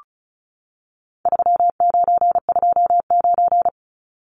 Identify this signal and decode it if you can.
Morse code, heard as a keyed tone, about 34 wpm, 700 Hz. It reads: 3929
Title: Numbers - 4-Digit 35wpm